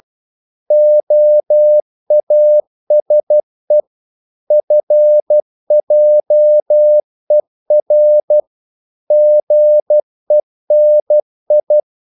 Morse code, heard as a keyed tone, about 12 wpm, 600 Hz. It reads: OASE FJER GENI
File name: Korte ord DK 12wpm